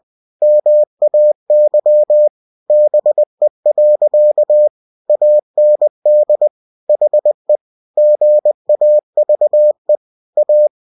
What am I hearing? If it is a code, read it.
Morse code, heard as a keyed tone, about 20 wpm, 600 Hz. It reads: MAY BE. AND HE GAVE A
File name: winnie-20wpm